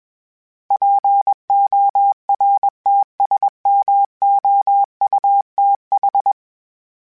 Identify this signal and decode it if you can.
Morse code, heard as a keyed tone, about 21 wpm, 800 Hz. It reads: PORTSMOUTH
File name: fromSMs / FromSMs mp3s / Cities@20